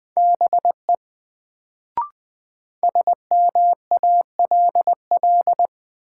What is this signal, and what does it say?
Morse code, heard as a keyed tone, about 21 wpm, 700 Hz. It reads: BE SMALL
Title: Sets of 2 Words - New Word Drill - 20wpm